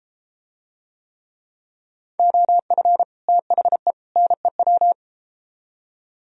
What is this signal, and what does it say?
Morse code, heard as a keyed tone, about 32 wpm, 700 Hz. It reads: OF THE NEW